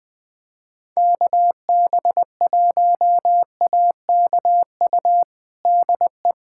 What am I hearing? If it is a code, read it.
Morse code, heard as a keyed tone, about 21 wpm, 700 Hz. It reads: KB1AKU DE
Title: USA Callsign DE Callsign - 6 Characters - 20wpm